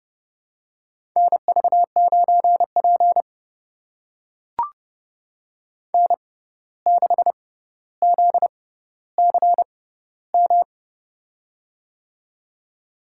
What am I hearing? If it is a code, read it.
Morse code, heard as a keyed tone, about 32 wpm, spaced as for 15 wpm, 700 Hz. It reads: NV9P N6ZCM
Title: USA Call Signs - 6x Character Spacing - 30wpm